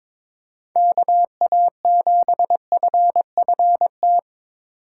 Morse code, heard as a keyed tone, about 22 wpm, 700 Hz. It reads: KA7FFT
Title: USA Call Signs - 6 Characters - 4x Character Spacing - 22wpm